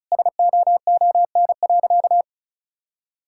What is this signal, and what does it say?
Morse code, heard as a keyed tone, about 36 wpm, 700 Hz. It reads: SOON.
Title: Sentences from Top 1000 Words - Review Part 6 - 35wpm